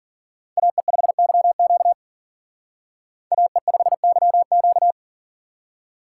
Morse code, heard as a keyed tone, about 47 wpm, 700 Hz. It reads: AE5YQ AE5YQ
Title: Speed Racing - USA 2x2 Call Signs 40wpm